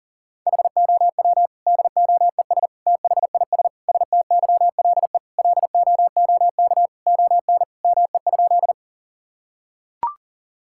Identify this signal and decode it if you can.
Morse code, heard as a keyed tone, about 40 wpm, 700 Hz. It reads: HOW DOES THIS STYLE LOOK ON ME?
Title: Sentences from Top 1000 Words - Review Part 5 - 40wpm